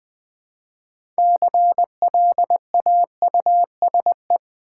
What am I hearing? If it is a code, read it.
Morse code, heard as a keyed tone, about 20 wpm, 700 Hz. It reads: CLAUSE
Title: Build A Word - 2 Letters At A Time - 6-Letter Words - 20wpm